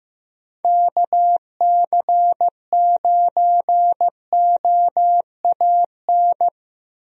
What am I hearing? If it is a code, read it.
Morse code, heard as a keyed tone, about 15 wpm, 700 Hz. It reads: KC9OAN
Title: USA Call Signs - 6 Characters - Repeated 3x - 15wpm